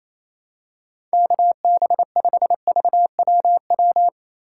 Morse code, heard as a keyed tone, about 28 wpm, 700 Hz. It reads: KB5VWW
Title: USA Call Signs - 6 Characters - 4x Character Spacing - 28wpm